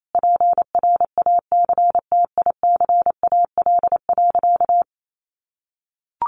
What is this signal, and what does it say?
Morse code, heard as a keyed tone, about 28 wpm, 700 Hz. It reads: PRACTICAL.
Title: Sentences from Top 1400 Words 28wpm